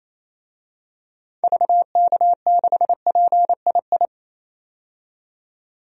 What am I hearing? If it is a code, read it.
Morse code, heard as a keyed tone, about 28 wpm, 700 Hz. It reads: VK6PII
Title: DXCC-100 Call Signs - 6 Characters - 28wpm